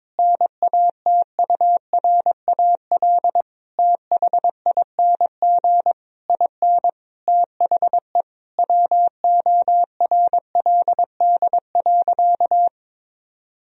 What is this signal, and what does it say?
Morse code, heard as a keyed tone, about 22 wpm, 700 Hz. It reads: NATURAL THING IN THE WORLD.
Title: Sentences from Top 1000 Words - Review Part 4 - 22wpm